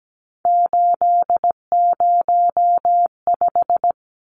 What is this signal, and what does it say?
Morse code, heard as a keyed tone, about 17 wpm, 700 Hz. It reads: 805
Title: Numbers - 3-Digit 17wpm